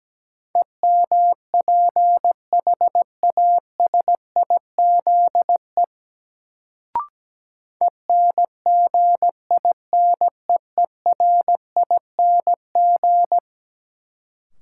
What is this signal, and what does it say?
Morse code, heard as a keyed tone, about 17 wpm, 700 Hz. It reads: EMPHASIZE ENGINEERING
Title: Sentences from Top 1500 Words - New Word Drill Alt - 17wpm